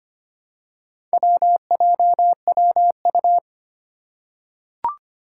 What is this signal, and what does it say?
Morse code, heard as a keyed tone, about 25 wpm, 700 Hz. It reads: WJWU
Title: Four Challenging Letter-Number Combinations 25wpm